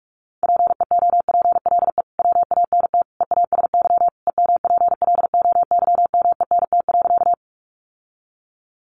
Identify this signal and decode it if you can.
Morse code, heard as a keyed tone, about 44 wpm, 700 Hz. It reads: PEOPLE WANT EASY EMPLOYMENT.
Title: Sentences from Top 1500 Words 45wpm